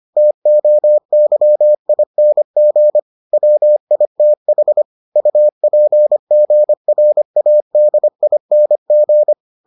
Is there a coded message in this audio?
Yes: Morse code from 0.2 to 9.3 s, heard as a keyed tone, about 25 wpm, 600 Hz. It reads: TOYING WITH UPGRADING